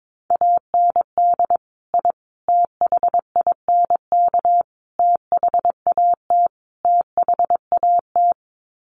Morse code, heard as a keyed tone, about 22 wpm, 700 Hz. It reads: AND I THINK THAT THAT
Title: Sets of 5 Words Encore 22wpm